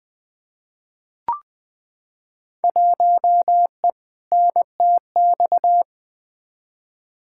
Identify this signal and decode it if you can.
Morse code, heard as a keyed tone, about 20 wpm, 700 Hz. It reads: 1E NTX
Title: Field Day Exchanges - 20wpm